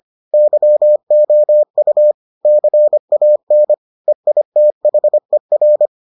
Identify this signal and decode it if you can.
Morse code, heard as a keyed tone, about 25 wpm, 600 Hz. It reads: YOU CAN EITHER